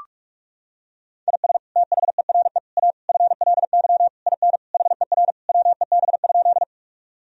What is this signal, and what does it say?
Morse code, heard as a keyed tone, about 41 wpm, 700 Hz. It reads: IS THERE A FLY IN HER WEB?
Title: Sentences from Top 1000 Words - Review Part 4 - 45wpm